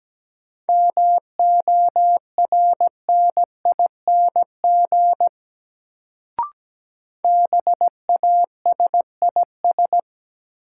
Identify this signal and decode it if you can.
Morse code, heard as a keyed tone, about 17 wpm, 700 Hz. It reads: MORNING BASIS
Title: Sets of 3 Words - New Word Drill - 17wpm